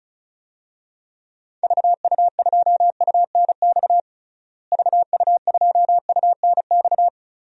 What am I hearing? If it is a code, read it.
Morse code, heard as a keyed tone, about 36 wpm, 700 Hz. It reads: VU2UNX VU2UNX
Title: DXCC-100 Call Signs - 6 Characters - Repeated 3x - 35wpm